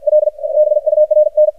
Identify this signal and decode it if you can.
Morse code, heard as a keyed tone, about 54 wpm, 610 Hz, with radio noise. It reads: L BURN
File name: archived / CRSnet